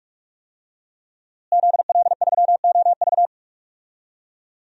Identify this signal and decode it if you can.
Morse code, heard as a keyed tone, about 48 wpm, 700 Hz. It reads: ZL3OV
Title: DXCC-100 Call Signs - 5 Characters - Repeated 3x - 45wpm